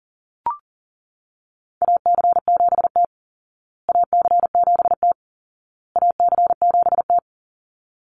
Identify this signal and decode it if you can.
Morse code, heard as a keyed tone, about 40 wpm, 700 Hz. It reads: AC7T AC7T AC7T
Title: Call Signs Repeated 2x - 40wpm